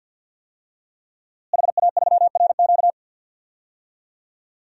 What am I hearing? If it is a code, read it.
Morse code, heard as a keyed tone, about 46 wpm, 700 Hz. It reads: HA3RQ
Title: DXCC-100 Call Signs - 5 Characters - Repeated 2x - 50wpm